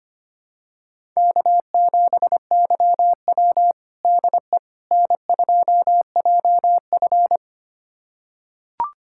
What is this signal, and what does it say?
Morse code, heard as a keyed tone, about 25 wpm, 700 Hz. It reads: K7YW DE N2JF
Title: USA Callsign DE Callsign - 4 Characters - 25wpm